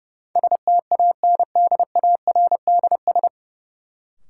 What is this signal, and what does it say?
Morse code, heard as a keyed tone, about 30 wpm, 700 Hz. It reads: STANDARDS
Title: Sentences from Top 1000 Words - Review Part 6 - New Word Drill - 30wpm